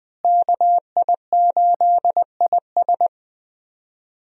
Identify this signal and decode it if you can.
Morse code, heard as a keyed tone, about 21 wpm, 700 Hz. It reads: KI8IS
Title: USA Call Signs - 5 Characters - Repeated 3x - 20wpm